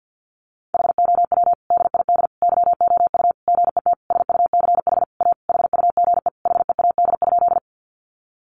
Morse code, heard as a keyed tone, about 50 wpm, 700 Hz. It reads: HOW DID YOU GET SUCH A HUGE HEAD?
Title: Sentences from Top 1100 Words 50wpm